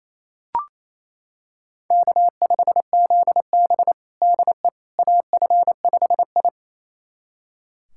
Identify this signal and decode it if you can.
Morse code, heard as a keyed tone, about 28 wpm, 700 Hz. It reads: K5ZB DE AF5I
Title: USA Callsign DE Callsign - 4 Characters - 28wpm